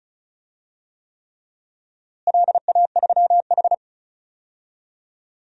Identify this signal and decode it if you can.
Morse code, heard as a keyed tone, about 36 wpm, 700 Hz. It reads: LA3H
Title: DXCC-100 Call Signs - 4 Characters - Repeated 3x - 35wpm